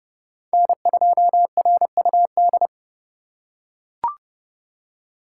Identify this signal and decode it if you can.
Morse code, heard as a keyed tone, about 30 wpm, 700 Hz. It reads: N2RUD
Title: USA Call Signs - 5 Characters - 2x Character Spacing - 30wpm